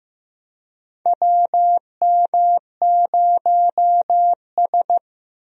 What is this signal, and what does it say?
Morse code, heard as a keyed tone, about 15 wpm, 700 Hz. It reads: WM0S
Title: USA Call Signs - 4 Characters - Repeated 3x - 15wpm